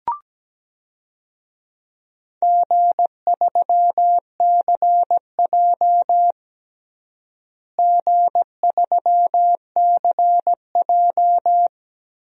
Morse code, heard as a keyed tone, about 17 wpm, 700 Hz. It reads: G3CJ G3CJ
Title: DXCC-100 Call Signs - 4 Characters - Repeated 3x - 17wpm